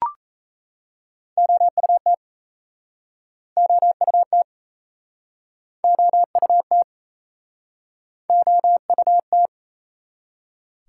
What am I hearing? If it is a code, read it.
Morse code, heard as a keyed tone, about 41 wpm, 700 Hz. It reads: OUT OUT OUT OUT